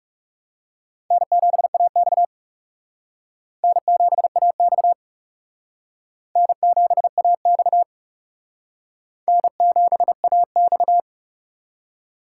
Morse code, heard as a keyed tone, about 36 wpm, 700 Hz. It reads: N7AX N7AX N7AX N7AX
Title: Speed Racing - USA 1x2 Call Signs 30wpm